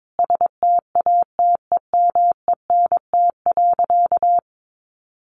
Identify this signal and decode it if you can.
Morse code, heard as a keyed tone, about 22 wpm, 700 Hz. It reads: STATEMENT.
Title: Sentences from Top 1200 Words 22wpm